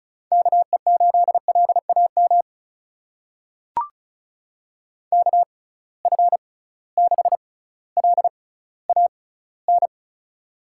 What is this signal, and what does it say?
Morse code, heard as a keyed tone, about 36 wpm, spaced as for 18 wpm, 700 Hz. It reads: KE8LAM KF6LAN
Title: USA Call Signs - 6 Characters - 6x Character Spacing - 35wpm